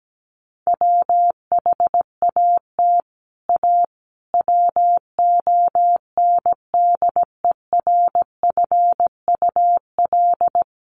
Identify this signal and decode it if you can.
Morse code, heard as a keyed tone, about 17 wpm, 700 Hz. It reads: WHAT A WONDERFUL
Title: Sentences from Top 1100 Words 17wpm